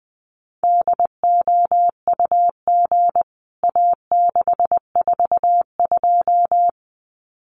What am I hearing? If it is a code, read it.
Morse code, heard as a keyed tone, about 19 wpm, 700 Hz. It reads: DOUG A642
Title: CWT Exchange 20wpm